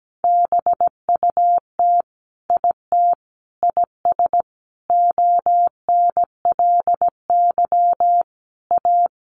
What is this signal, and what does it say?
Morse code, heard as a keyed tone, about 17 wpm, 700 Hz. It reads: BUT IT IS ONLY A
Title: Sentences from Top 1500 Words 17wpm